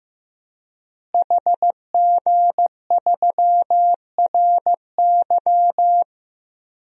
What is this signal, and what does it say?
Morse code, heard as a keyed tone, about 15 wpm, 700 Hz. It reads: HG3RY
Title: DXCC-100 Call Signs - 5 Characters - Repeated 3x - 15wpm